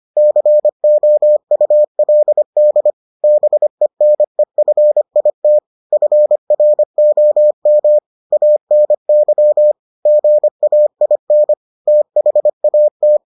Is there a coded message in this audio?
Yes: Morse code from 0.2 to 13.2 s, heard as a keyed tone, about 25 wpm, 600 Hz. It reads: COULD BENEFIT FROM ANY GAIN THAT